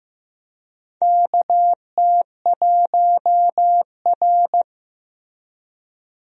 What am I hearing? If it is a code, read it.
Morse code, heard as a keyed tone, about 15 wpm, 700 Hz. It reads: KT1R
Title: USA Call Signs - 4 Characters - Repeated 2x - 15wpm